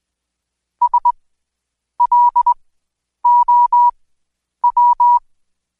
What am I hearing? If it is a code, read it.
Morse code, heard as a keyed tone, about 20 wpm, spaced as for 9 wpm, 970 Hz. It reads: SLOW
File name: Code Quick / AudioDownld2 / Seg115-Day20-DayDrill 20